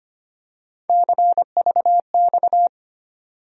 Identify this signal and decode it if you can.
Morse code, heard as a keyed tone, about 25 wpm, 700 Hz. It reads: CVX
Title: USA Call Sign Postfixes - 3 Letters 25wpm